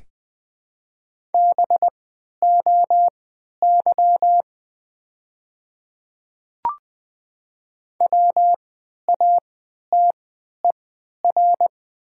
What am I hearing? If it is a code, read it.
Morse code, heard as a keyed tone, about 20 wpm, spaced as for 11 wpm, 700 Hz. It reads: BOY WATER
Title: 3x Character Spacing - Top 200 Words - 20wpm